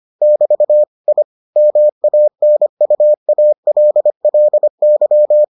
Morse code, heard as a keyed tone, about 25 wpm, 600 Hz. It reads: <BT> I MANUALLY